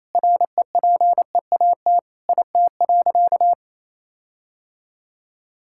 Morse code, heard as a keyed tone, about 28 wpm, 700 Hz. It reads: REPEAT IT.
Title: Sentences from Top 1600 Words 28wpm